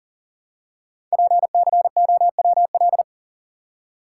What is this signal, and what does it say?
Morse code, heard as a keyed tone, about 39 wpm, 700 Hz. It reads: PCOWL
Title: Code Groups - 5 Letters - 40wpm